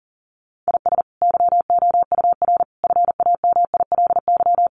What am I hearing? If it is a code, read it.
Morse code, heard as a keyed tone, about 40 wpm, 700 Hz. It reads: IS YOUR FAMILY
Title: Sentences from Top 400 Words 40wpm